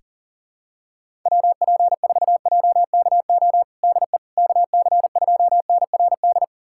Morse code, heard as a keyed tone, about 40 wpm, 700 Hz. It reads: WP4JKO DE KC2NRD
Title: USA Callsign DE Callsign 40wpm